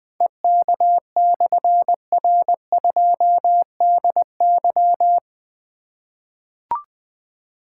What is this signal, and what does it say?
Morse code, heard as a keyed tone, about 20 wpm, 700 Hz. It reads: EK/R2DY
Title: Difficult Call Signs 20wpm